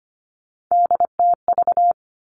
Morse code, heard as a keyed tone, about 25 wpm, 700 Hz. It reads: DTV
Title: Three Challenging Letter-Number Combinations 25wpm